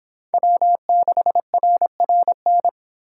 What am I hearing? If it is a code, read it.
Morse code, heard as a keyed tone, about 26 wpm, 700 Hz. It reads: W6RRN